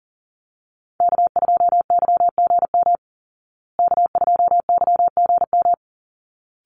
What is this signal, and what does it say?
Morse code, heard as a keyed tone, about 40 wpm, 700 Hz. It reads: K2YGM K2YGM
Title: Call Signs Repeated 3x - 40wpm